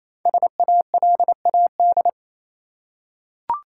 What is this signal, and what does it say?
Morse code, heard as a keyed tone, about 28 wpm, 700 Hz. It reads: SALAD
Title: Five Letter Words - 28wpm